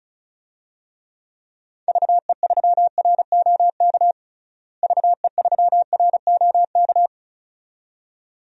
Morse code, heard as a keyed tone, about 34 wpm, 700 Hz. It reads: VE3ROK VE3ROK
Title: DXCC-100 Call Signs - 6 Characters - Repeated 2x - 35wpm